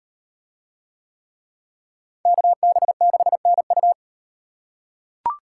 Morse code, heard as a keyed tone, about 38 wpm, 700 Hz. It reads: KB6NU